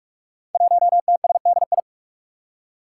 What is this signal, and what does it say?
Morse code, heard as a keyed tone, about 46 wpm, 700 Hz. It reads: 1TSDI
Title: Five Challenging Letter-Number Combinations 45wpm